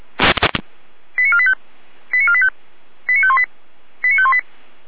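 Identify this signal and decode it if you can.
Five-tone selective call (ZVEI), heard as quick runs of five tones, tones of about 70 ms. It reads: 89485 89485 89518 89518
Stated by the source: ZVEI five-tone selective call 89402, received by SDR.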